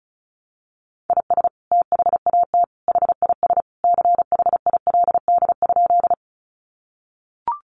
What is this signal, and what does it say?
Morse code, heard as a keyed tone, about 34 wpm, 700 Hz. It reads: IS THAT HIS CHILD?
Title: Sentences from Top 800 Words 35wpm